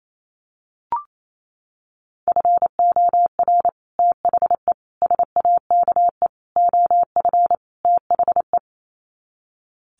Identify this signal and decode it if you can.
Morse code, heard as a keyed tone, about 28 wpm, 700 Hz. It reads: FOR THE SAKE OF THE
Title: Sets of 5 Words Encore 28wpm